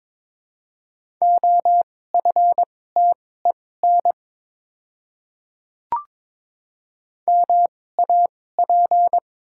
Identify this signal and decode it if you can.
Morse code, heard as a keyed tone, about 22 wpm, spaced as for 16 wpm, 700 Hz. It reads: OFTEN MAP
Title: 2x Character Spacing - Top 500 Words - 22wpm